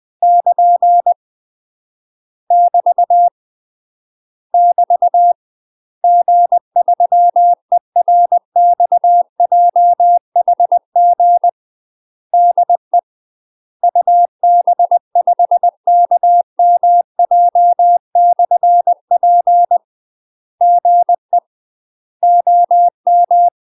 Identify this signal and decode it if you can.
Morse code, heard as a keyed tone, about 20 wpm, 700 Hz. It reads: <KN> <BT> <BT> G3ERXJHG DE UB5KMJ/P GE OM
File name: UK MASTER QSOs TRAINING / 20WPM / UK MASTER QSOs 0024